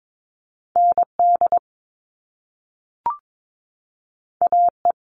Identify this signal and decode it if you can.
Morse code, heard as a keyed tone, about 22 wpm, 700 Hz. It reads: ND AE
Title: Top Two Letter Combinations in Words 22wpm